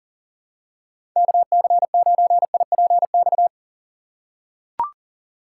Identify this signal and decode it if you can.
Morse code, heard as a keyed tone, about 40 wpm, 700 Hz. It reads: KC9IPX
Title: USA Call Signs - 6 Characters - 4x Character Spacing - 40wpm